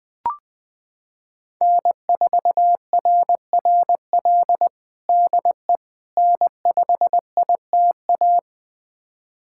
Morse code, heard as a keyed tone, about 20 wpm, 700 Hz. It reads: N4RRL DE N5ITA
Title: USA Callsign DE Callsign - 5 Characters - 20wpm